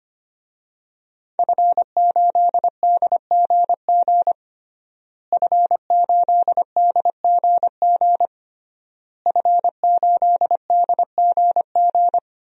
Morse code, heard as a keyed tone, about 25 wpm, 700 Hz. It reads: F8DGG F8DGG F8DGG
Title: DXCC-100 Call Signs - 5 Characters - Repeated 3x - 25wpm